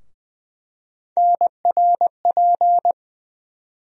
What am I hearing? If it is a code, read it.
Morse code, heard as a keyed tone, about 20 wpm, 700 Hz. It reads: NRP
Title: USA Call Sign Postfixes - 3 Letters 20wpm